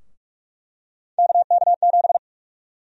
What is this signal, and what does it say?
Morse code, heard as a keyed tone, about 47 wpm, 700 Hz. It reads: KK7
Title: USA Call Sign Prefixes - 45wpm